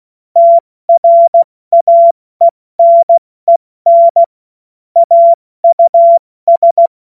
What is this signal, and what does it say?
Morse code, heard as a keyed tone, about 16 wpm, 680 Hz. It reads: TRAENEN AUS
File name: Morsekurs / Witze / Witz 06. September 2022 70 F80